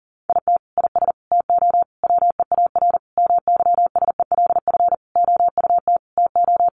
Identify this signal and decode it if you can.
Morse code, heard as a keyed tone, about 40 wpm, 700 Hz. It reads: IT IS TO WEAR MYSELF OUT TO
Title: Sentences from Top 1500 Words 40wpm